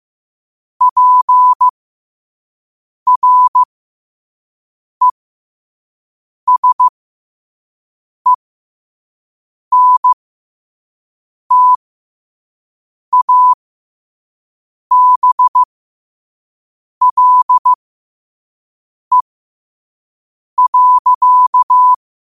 Morse code, heard as a keyed tone, about 15 wpm, spaced as for 5 wpm, 1000 Hz. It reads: PRESENTABLE.
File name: qotdfiles / 5wpm